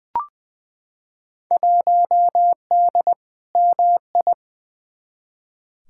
Morse code, heard as a keyed tone, about 20 wpm, 700 Hz. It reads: 1D MI
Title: Field Day Exchanges - 20wpm